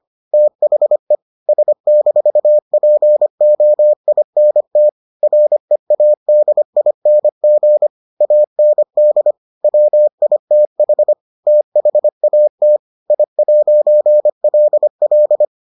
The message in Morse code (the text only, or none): THE S-POINT READING AND WITH THAT I'LL